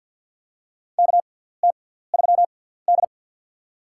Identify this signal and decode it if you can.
Morse code, heard as a keyed tone, about 47 wpm, spaced as for 17 wpm, 690 Hz. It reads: KT3D
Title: USA Call Signs - 6x Character Spacing - 50wpm